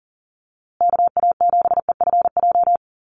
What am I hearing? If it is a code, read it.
Morse code, heard as a keyed tone, about 40 wpm, 700 Hz. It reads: KA7EFJ
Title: USA Call Signs Repeated 2x 40wpm